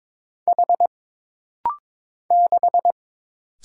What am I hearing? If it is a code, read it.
Morse code, heard as a keyed tone, about 22 wpm, 700 Hz. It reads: H 6
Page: This rapid-fire timing is geared to push your ICR (Instant Character Recognition) skills.